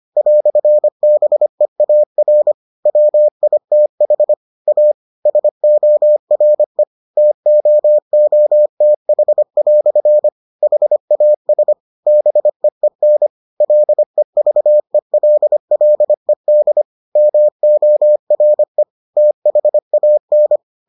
Morse code, heard as a keyed tone, about 25 wpm, 600 Hz. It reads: "BEAR WITH A SORE TOOTH" HAS BEEN LEVELLED MORE THAN